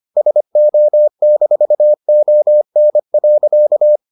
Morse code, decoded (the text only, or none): SO-ON.